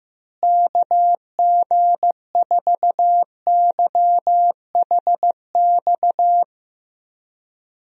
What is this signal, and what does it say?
Morse code, heard as a keyed tone, about 15 wpm, 700 Hz. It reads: KG4YHX
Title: USA Call Signs - 6 Characters - Repeated 3x - 15wpm